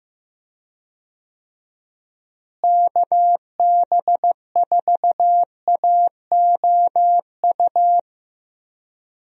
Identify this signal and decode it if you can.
Morse code, heard as a keyed tone, about 15 wpm, 700 Hz. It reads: KB4AOU
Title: USA Call Signs - 6 Characters - Repeated 3x - 15wpm